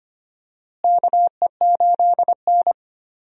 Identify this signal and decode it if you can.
Morse code, heard as a keyed tone, about 25 wpm, 700 Hz. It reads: KE8N
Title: USA Call Signs - 4 Characters - Repeated 3x - 25wpm